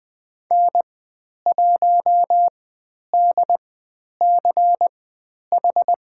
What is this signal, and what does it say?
Morse code, heard as a keyed tone, about 20 wpm, spaced as for 10 wpm, 700 Hz. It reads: N1DCH
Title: 109 Lesson 18g - Letter m - Call Signs 20wpm